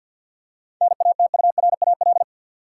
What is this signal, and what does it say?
Morse code, heard as a keyed tone, about 53 wpm, 700 Hz. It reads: NATURAL
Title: Seven Letter Words - 50wpm